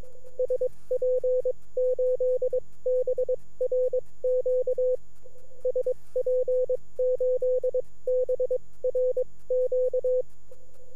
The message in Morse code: SP8BRQ SP8BRQ